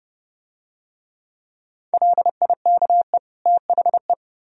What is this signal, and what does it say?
Morse code, heard as a keyed tone, about 28 wpm, 700 Hz. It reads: LIKE THE